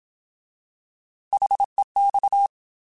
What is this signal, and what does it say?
Morse code, heard as a keyed tone, about 27 wpm, 800 Hz. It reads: HEX
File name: Alphabetical / Alphabetical mp3s / Hs @25